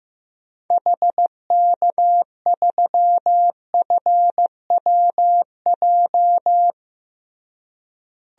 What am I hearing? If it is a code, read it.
Morse code, heard as a keyed tone, about 15 wpm, 700 Hz. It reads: HK3FWJ
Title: DXCC-100 Call Signs - 6 Characters - Repeated 2x - 15wpm